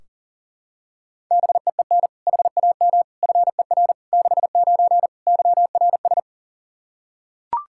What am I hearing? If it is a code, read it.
Morse code, heard as a keyed tone, about 40 wpm, 700 Hz. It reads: BEEN HAM FER 69 YRS
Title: Practice QSO - Exchange 3 - More Info - 40wpm